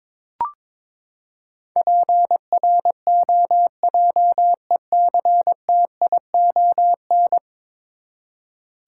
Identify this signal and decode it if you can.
Morse code, heard as a keyed tone, about 22 wpm, 700 Hz. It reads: PROJECTION
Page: morse-code-ninja-practice-22wpm